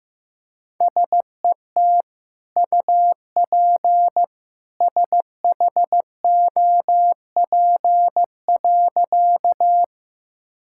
Morse code, heard as a keyed tone, about 15 wpm, 700 Hz. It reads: SET UP SHOP.
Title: Sentences from Top 1600 Words 15wpm